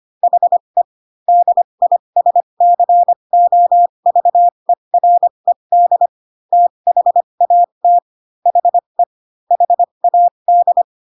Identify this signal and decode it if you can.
Morse code, heard as a keyed tone, about 25 wpm, spaced as for 22 wpm, 700 Hz. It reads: HE DISCOVERED THAT HE HAD
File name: Pinocchio / Pinocchio ch3 18 25 ik1hge